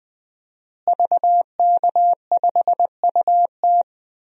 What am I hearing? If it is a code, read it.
Morse code, heard as a keyed tone, about 20 wpm, 700 Hz. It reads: VK5UT
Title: DXCC-100 Call Signs - 5 Characters - Repeated 2x - 20wpm